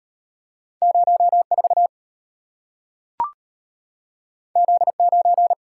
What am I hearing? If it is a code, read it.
Morse code, heard as a keyed tone, about 33 wpm, 690 Hz. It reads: 04 Z9